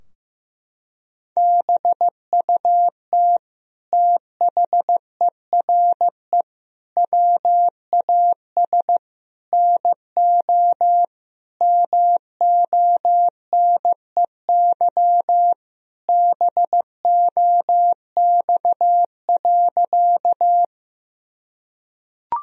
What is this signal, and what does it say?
Morse code, heard as a keyed tone, about 15 wpm, 700 Hz. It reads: BUT THERE WAS NO MONEY BOX.
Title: Sentences from Top 1000 Words - Review Part 2 - 15wpm